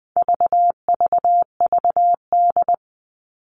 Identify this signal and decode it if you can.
Morse code, heard as a keyed tone, about 20 wpm, 700 Hz. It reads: VVVD
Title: All Cut Numbers - 4-Digit 20wpm